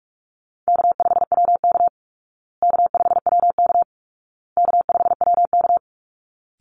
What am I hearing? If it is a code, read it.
Morse code, heard as a keyed tone, about 44 wpm, 700 Hz. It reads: K5WK K5WK K5WK
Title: Easy Call Signs Repeated 3x - 45wpm